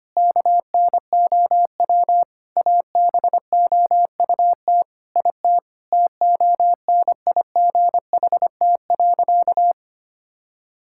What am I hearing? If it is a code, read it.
Morse code, heard as a keyed tone, about 25 wpm, 700 Hz. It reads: KNOW ABOUT IT TONIGHT.
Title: Sentences from Top 1000 Words - Review Part 5 - 25wpm